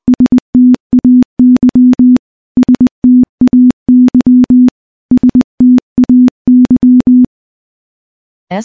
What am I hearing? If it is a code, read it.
Morse code, heard as a keyed tone, about 20 wpm, 250 Hz. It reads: STAY STAY STAY
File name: morse-training-20WPM